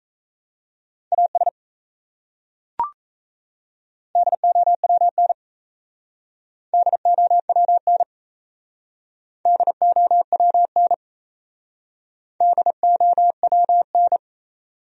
Morse code, heard as a keyed tone, about 34 wpm, 700 Hz. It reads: AS DOWN DOWN DOWN DOWN
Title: Speed-Racing Top 100 Words 28wpm